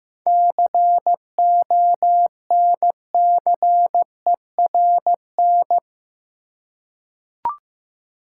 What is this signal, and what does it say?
Morse code, heard as a keyed tone, about 15 wpm, 700 Hz. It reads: CONCERN
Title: Seven Letter Words - 15wpm